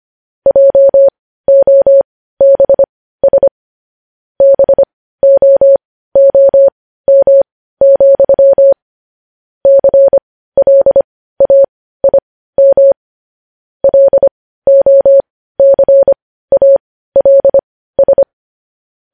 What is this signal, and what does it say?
Morse code, heard as a keyed tone, about 25 wpm, spaced as for 15 wpm, 550 Hz. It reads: JOBS BOOM, CLAIM LOCALS
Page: news-headlines-in-morse-code-at-15-wpm